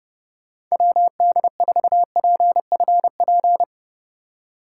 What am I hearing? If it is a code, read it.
Morse code, heard as a keyed tone, about 38 wpm, 700 Hz. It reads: WD4PFP